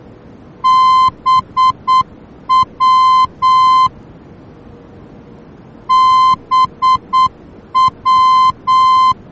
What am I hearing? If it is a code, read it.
Morse code, heard as a keyed tone, about 8 wpm, 1040 Hz. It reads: BW BW
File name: germany / mx / BW-D